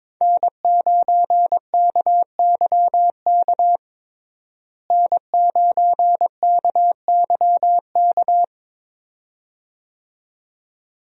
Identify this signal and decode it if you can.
Morse code, heard as a keyed tone, about 22 wpm, 700 Hz. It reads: N9KYK N9KYK
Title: USA Call Signs - 5 Characters - Repeated 2x - 22wpm